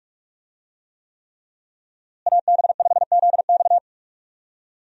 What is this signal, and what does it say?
Morse code, heard as a keyed tone, about 46 wpm, 700 Hz. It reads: AB5ZX
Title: USA Call Signs - 5 Characters - Repeated 3x - 45wpm